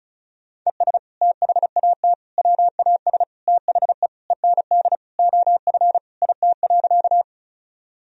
Morse code, heard as a keyed tone, about 36 wpm, 700 Hz. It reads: ES THAT WAS THE END OF IT.
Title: Sentences from Top 200 Words and Abbreviations 35wpm